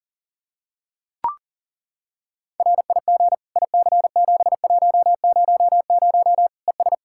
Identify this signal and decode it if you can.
Morse code, heard as a keyed tone, about 37 wpm, 700 Hz. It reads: RIG IC7100 ES
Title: Practice QSO - Exchange 2 - Info - 40wpm